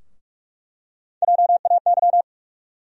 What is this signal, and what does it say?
Morse code, heard as a keyed tone, about 48 wpm, 700 Hz. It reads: JAY
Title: USA Call Sign Postfixes - 3 Letters 45wpm